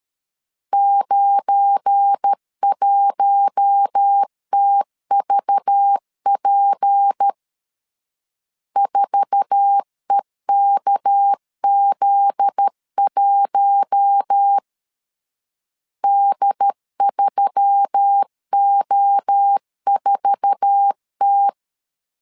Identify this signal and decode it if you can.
Morse code, heard as a keyed tone, about 13 wpm, 800 Hz. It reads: 91TVP 4EKZ1 D3O4T
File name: LEZIONI / 8 YJ45 / 8 YJ4 12wpm 2151